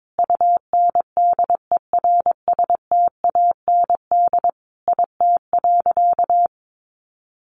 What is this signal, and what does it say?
Morse code, heard as a keyed tone, about 22 wpm, 700 Hz. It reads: UNDERSTAND IT.
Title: Sentences from Top 600 Words 22wpm